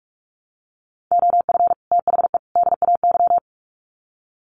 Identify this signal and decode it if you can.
Morse code, heard as a keyed tone, about 44 wpm, 700 Hz. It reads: OF THE DAY
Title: Sets of 3 Words 45wpm